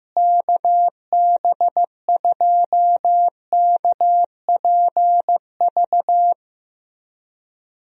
Morse code, heard as a keyed tone, about 16 wpm, 700 Hz. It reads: KB2KPV
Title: USA Call Signs - 6 Characters - Repeated 3x - 15wpm